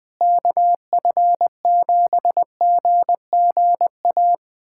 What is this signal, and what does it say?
Morse code, heard as a keyed tone, about 20 wpm, 700 Hz. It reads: KF7GGA
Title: USA Callsign DE Callsign 20wpm